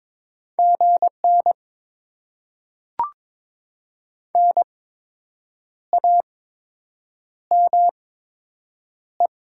Morse code, heard as a keyed tone, about 22 wpm, spaced as for 7 wpm, 700 Hz. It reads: GN NAME
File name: Top 90 QSO Elements - 8x Character Spacing - 22wpm